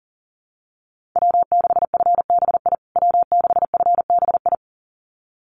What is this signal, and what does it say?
Morse code, heard as a keyed tone, about 37 wpm, 700 Hz. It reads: W6FBI W6FBI
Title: USA Call Signs Repeated 2x 40wpm